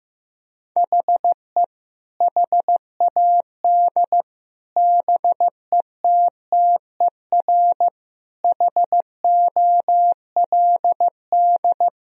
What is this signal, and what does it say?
Morse code, heard as a keyed tone, about 16 wpm, 700 Hz. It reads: HE HAD BETTER HOLD
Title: Sentences from Top 1000 Words - Review Part 2 - 15wpm